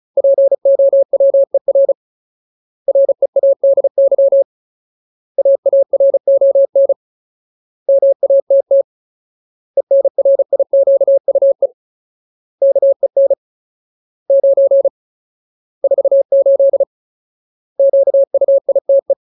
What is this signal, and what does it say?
Morse code, heard as a keyed tone, about 34 wpm, 550 Hz. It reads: POWER READY AARON MATT ENRIQUE KEN 9 48 QUITE
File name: Ultimate Word List / 35WPM Ultimate